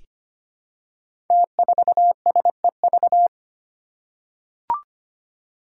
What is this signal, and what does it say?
Morse code, heard as a keyed tone, about 25 wpm, 700 Hz. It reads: T4SEV
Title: Five Challenging Letter-Number Combinations 25wpm